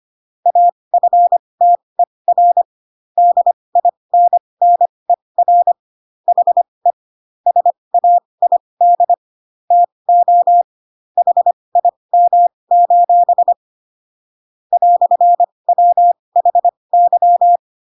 Morse code, heard as a keyed tone, about 25 wpm, spaced as for 20 wpm, 700 Hz. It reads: AFTER DINNER HE SAID TO HIM: "WHY
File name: Pinocchio / Pinocchio ch8 15 25 ik1hge